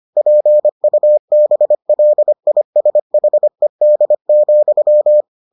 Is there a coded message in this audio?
Yes: Morse code from 0.2 to 5.2 s, heard as a keyed tone, about 25 wpm, 600 Hz. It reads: PUBLISHED,